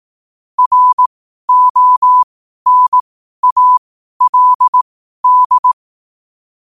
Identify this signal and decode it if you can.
Morse code, heard as a keyed tone, about 18 wpm, spaced as for 12 wpm, 1000 Hz. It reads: RONALD
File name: qotdfiles / 12wpm